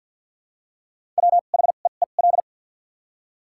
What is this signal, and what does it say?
Morse code, heard as a keyed tone, about 53 wpm, 700 Hz. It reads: W H E E L
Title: 2x Character Spacing - Top 500 Words - 50wpm